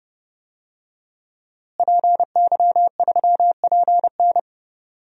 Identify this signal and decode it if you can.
Morse code, heard as a keyed tone, about 30 wpm, 700 Hz. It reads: PY3PN
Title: DXCC-100 Call Signs - 5 Characters - Repeated 2x - 30wpm